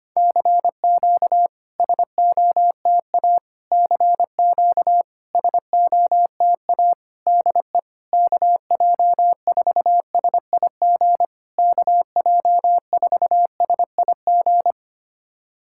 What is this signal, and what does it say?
Morse code, heard as a keyed tone, about 25 wpm, 700 Hz. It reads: CQ SOTA CQ SOTA DE KJ4SIG KJ4SIG
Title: SOTA QSO Practice 25wpm